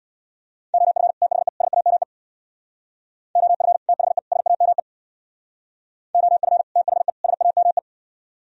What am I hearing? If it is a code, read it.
Morse code, heard as a keyed tone, about 64 wpm, 700 Hz. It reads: OF THE SAME OF THE SAME OF THE SAME